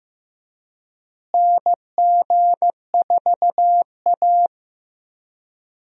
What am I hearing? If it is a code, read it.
Morse code, heard as a keyed tone, about 15 wpm, 700 Hz. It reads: NG4A
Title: USA Call Signs - 4 Characters - Repeated 3x - 15wpm